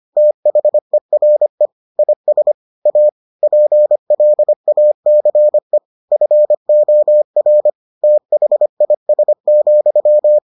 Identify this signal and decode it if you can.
Morse code, heard as a keyed tone, about 25 wpm, 600 Hz. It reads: THERE IS A PLACE FOR THIS,